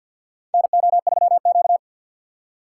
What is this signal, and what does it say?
Morse code, heard as a keyed tone, about 53 wpm, 700 Hz. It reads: NO3Q
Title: USA Call Signs - 4 Characters - 2x Character Spacing - 50wpm